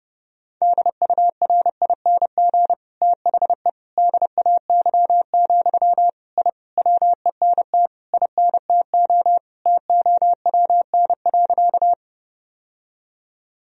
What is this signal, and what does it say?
Morse code, heard as a keyed tone, about 30 wpm, 700 Hz. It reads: DURING THE DAY, I WENT INTO TOWN.
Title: Sentences from Top 1000 Words - Review Part 2 - 30wpm